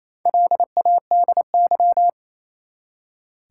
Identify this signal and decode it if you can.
Morse code, heard as a keyed tone, about 28 wpm, 700 Hz. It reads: LADY
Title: Four Letter Words - 28wpm